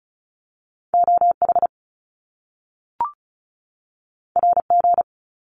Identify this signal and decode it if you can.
Morse code, heard as a keyed tone, about 36 wpm, 700 Hz. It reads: OH RG
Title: Top Two Letter Combinations in Words 35wpm